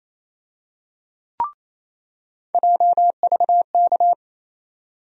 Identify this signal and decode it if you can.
Morse code, heard as a keyed tone, about 28 wpm, 700 Hz. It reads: JVK
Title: USA Call Sign Postfixes - 3 Letters 28wpm